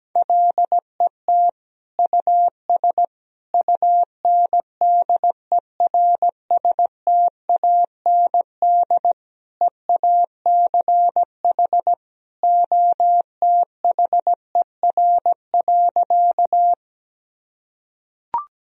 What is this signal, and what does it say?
Morse code, heard as a keyed tone, about 17 wpm, 700 Hz. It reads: LET US UNDERSTAND EACH OTHER.
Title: Sentences from Top 1000 Words - Review Part 1 - 17wpm